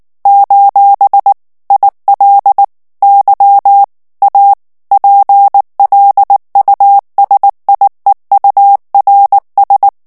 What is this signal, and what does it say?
Morse code, heard as a keyed tone, about 19 wpm, 800 Hz. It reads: : IL Y A PLUSIEURS
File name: FRANCAIS TEXTE CLAIR 1 VITESSE 18 MOTS MINUTES